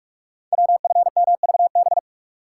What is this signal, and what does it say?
Morse code, heard as a keyed tone, about 44 wpm, 690 Hz. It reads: WFMVB
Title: Code Groups - 5 Letters - 45wpm